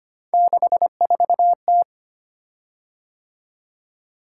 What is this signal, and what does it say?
Morse code, heard as a keyed tone, about 25 wpm, 700 Hz. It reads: 64T
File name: Three Letter-Number Combinations - 25wpm